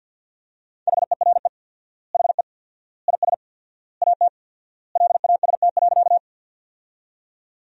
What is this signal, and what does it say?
Morse code, heard as a keyed tone, about 51 wpm, 700 Hz. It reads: HERE HE IS AT LAST.
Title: 4x Word-Spacing - Sentences from Top 300 Words - 50wpm